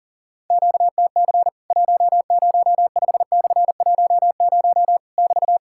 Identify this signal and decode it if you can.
Morse code, heard as a keyed tone, about 40 wpm, 700 Hz. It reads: QTC 105/10 <BT>
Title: Worked All Europe DX Contest - QTC Exchange - 40wpm